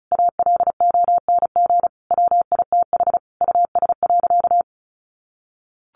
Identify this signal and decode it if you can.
Morse code, heard as a keyed tone, about 34 wpm, 700 Hz. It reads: ALONG WITH US.
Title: Sentences from Top 1200 Words 35wpm